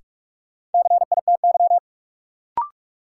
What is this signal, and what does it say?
Morse code, heard as a keyed tone, about 42 wpm, 700 Hz. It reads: CITY
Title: Sets of 2 Words - New Word Drill - 45wpm